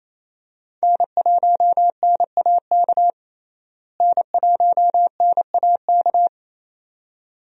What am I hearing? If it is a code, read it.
Morse code, heard as a keyed tone, about 28 wpm, 700 Hz. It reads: N1NAK N1NAK
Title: USA Call Signs - 5 Characters - Repeated 2x - 28wpm